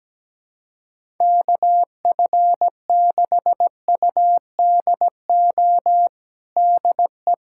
Morse code, heard as a keyed tone, about 17 wpm, 700 Hz. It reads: KF6UDO DE
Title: USA Callsign DE Callsign 17wpm